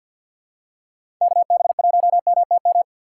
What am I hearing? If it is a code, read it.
Morse code, heard as a keyed tone, about 48 wpm, 700 Hz. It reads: KB1MTM
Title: USA Call Signs - 6 Characters - 6x Character Spacing - 50wpm